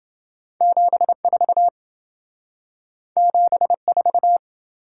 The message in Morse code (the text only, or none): 74 74